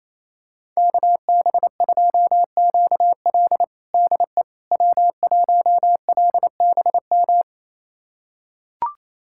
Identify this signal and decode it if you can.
Morse code, heard as a keyed tone, about 28 wpm, 700 Hz. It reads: KB2QL DE W1LBM
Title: USA Callsign DE Callsign - 5 Characters - 28wpm